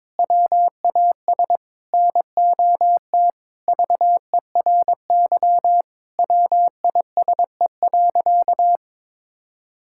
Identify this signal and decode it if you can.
Morse code, heard as a keyed tone, about 22 wpm, 700 Hz. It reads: WAS NOT VERY WISE.
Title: Sentences from Top 1600 Words 22wpm